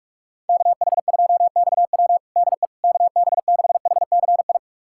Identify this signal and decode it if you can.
Morse code, heard as a keyed tone, about 46 wpm, 700 Hz. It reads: KH2XW DE KB6HCI
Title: USA Callsign DE Callsign 45wpm